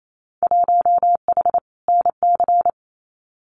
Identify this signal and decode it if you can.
Morse code, heard as a keyed tone, about 28 wpm, 700 Hz. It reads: 1H NC
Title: Winter Field Day Exchanges 28wpm